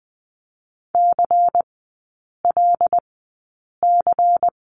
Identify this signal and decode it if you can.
Morse code, heard as a keyed tone, about 20 wpm, spaced as for 8 wpm, 700 Hz. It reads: CLC
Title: 093 Lesson 16e - Letter c - Three Characters 20wpm